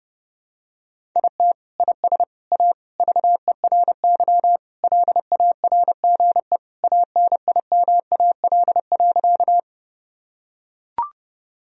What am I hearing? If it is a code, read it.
Morse code, heard as a keyed tone, about 32 wpm, 700 Hz. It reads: IT IS A VERY LARGE ANIMAL.
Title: Sentences from Top 1000 Words - Review Part 1 - 30wpm